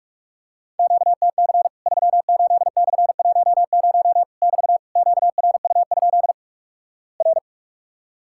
Morse code, heard as a keyed tone, about 46 wpm, 700 Hz. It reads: QTC 38/10 <BT> QRU? R
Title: Worked All Europe DX Contest - QTC Exchange - 45wpm